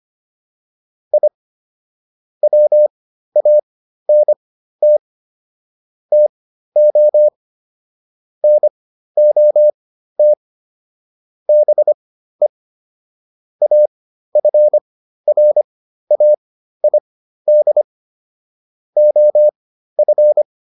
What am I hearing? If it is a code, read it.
Morse code, heard as a keyed tone, about 25 wpm, spaced as for 13 wpm, 600 Hz. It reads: I WANT TO NOT BE AFRAID OF